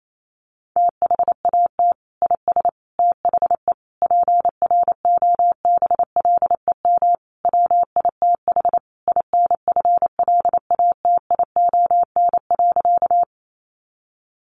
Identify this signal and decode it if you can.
Morse code, heard as a keyed tone, about 28 wpm, 700 Hz. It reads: THAT IS THE PROBLEM WITH INFLATION.
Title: Sentences from Top 1100 Words 28wpm